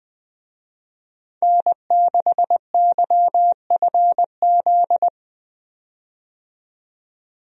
Morse code, heard as a keyed tone, about 20 wpm, 700 Hz. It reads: N6YFZ
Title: USA Call Signs - 5 Characters - 20wpm